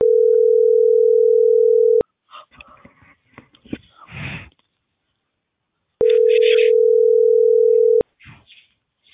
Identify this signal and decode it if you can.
Ringback tone (North America), 440 + 480 Hz.